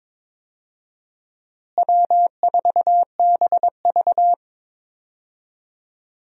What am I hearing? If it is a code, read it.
Morse code, heard as a keyed tone, about 22 wpm, 700 Hz. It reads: W4BV
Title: USA Call Signs - 4 Characters - Repeated 2x - 22wpm